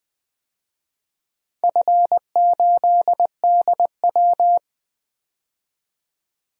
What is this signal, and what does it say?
Morse code, heard as a keyed tone, about 20 wpm, 700 Hz. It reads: F8DW
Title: DXCC-100 Call Signs - 4 Characters - 20wpm